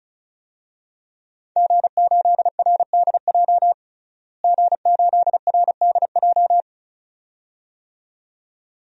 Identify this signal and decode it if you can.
Morse code, heard as a keyed tone, about 34 wpm, 700 Hz. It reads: G8RDJ G8RDJ
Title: DXCC-100 Call Signs - 5 Characters - Repeated 2x - 35wpm